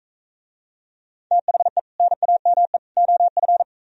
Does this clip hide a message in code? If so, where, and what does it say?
Yes: Morse code from 1.3 to 3.6 s, heard as a keyed tone, about 44 wpm, 700 Hz. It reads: THE NAME OF